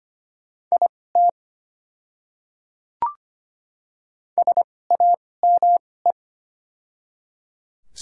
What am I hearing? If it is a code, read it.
Morse code, heard as a keyed tone, about 25 wpm, spaced as for 18 wpm, 700 Hz. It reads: IT SAME
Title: 2x Character Spacing - Top 200 Words - 25wpm